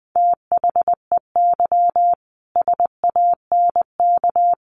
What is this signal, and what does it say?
Morse code, heard as a keyed tone, about 20 wpm, 700 Hz. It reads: THEY SANK
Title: A Sound Of Thunder 20wpm